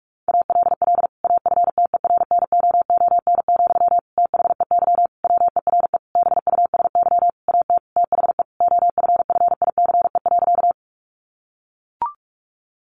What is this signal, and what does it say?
Morse code, heard as a keyed tone, about 44 wpm, 700 Hz. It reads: ALL AFTERNOON, THEY WERE BUSY AT THE OFFICE.
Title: Sentences from Top 1500 Words 45wpm